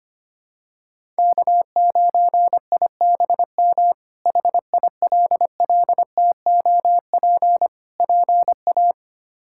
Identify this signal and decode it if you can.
Morse code, heard as a keyed tone, about 25 wpm, 700 Hz. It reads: K9IBM HILLTOP PA
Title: K1USN SST QSO Practice - 25wpm